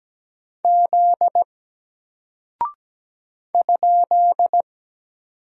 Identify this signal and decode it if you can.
Morse code, heard as a keyed tone, about 17 wpm, 700 Hz. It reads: Z ?
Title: Single Character (Letter:Number:Punctuation:Prosign) 17wpm